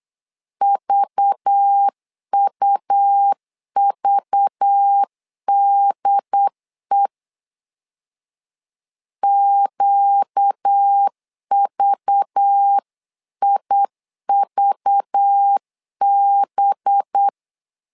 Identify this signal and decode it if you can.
Morse code, heard as a keyed tone, about 8 wpm, 800 Hz. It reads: VUVDE QVIVB
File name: LEZIONI / 5 LBVC / 5 LBV 8wpm 2837